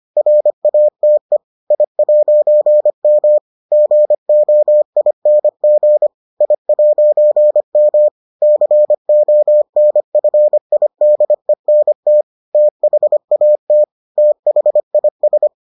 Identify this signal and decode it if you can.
Morse code, heard as a keyed tone, about 25 wpm, 600 Hz. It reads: RATE I'M GOING I'M CONFIDENT THAT THIS